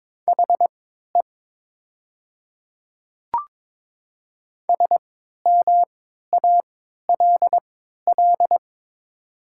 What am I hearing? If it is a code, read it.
Morse code, heard as a keyed tone, about 22 wpm, spaced as for 12 wpm, 690 Hz. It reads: HE SMALL
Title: 3x Character Spacing - Top 200 Words - 22wpm